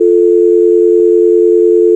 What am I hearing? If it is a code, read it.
Dial tone (North America), 350 + 440 Hz.
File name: SOUND EFFECTS and NOISES / Phone Tones / 90[kb]dial-tone.wav